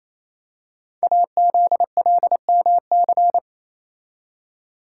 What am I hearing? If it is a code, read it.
Morse code, heard as a keyed tone, about 27 wpm, 700 Hz. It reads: AZLMC
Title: Code Groups - 5 Letters - 28wpm